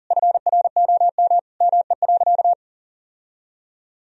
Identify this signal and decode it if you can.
Morse code, heard as a keyed tone, about 40 wpm, 700 Hz. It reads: FROM ME.
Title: Sentences from Top 1000 Words - Review Part 2 - 40wpm